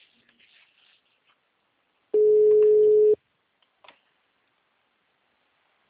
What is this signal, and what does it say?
Ringback tone (Europe, 425 Hz).